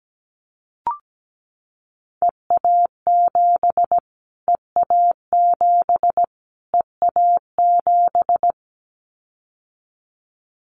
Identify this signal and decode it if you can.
Morse code, heard as a keyed tone, about 17 wpm, 700 Hz. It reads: EA7 EA7 EA7
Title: Callsign Prefixes for Top 25 DXCC - Repeated 3x - 17wpm